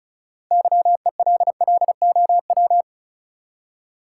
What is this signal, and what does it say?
Morse code, heard as a keyed tone, about 36 wpm, 700 Hz. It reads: YELLOW
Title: Six Letter Words - 35wpm